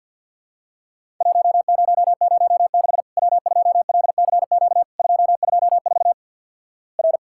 Morse code, heard as a keyed tone, about 50 wpm, 700 Hz. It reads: 1006 W2LCQ 224 R
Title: Worked All Europe DX Contest - QTC Exchange with Repeats - 50wpm